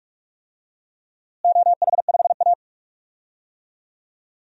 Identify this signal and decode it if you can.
Morse code, heard as a keyed tone, about 45 wpm, 700 Hz. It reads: OH5A
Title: DXCC-100 Call Signs - 4 Characters - Repeated 3x - 45wpm